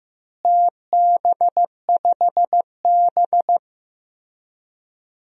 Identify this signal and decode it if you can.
Morse code, heard as a keyed tone, about 15 wpm, 700 Hz. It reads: TB5B
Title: DXCC-100 Call Signs - 4 Characters - Repeated 2x - 15wpm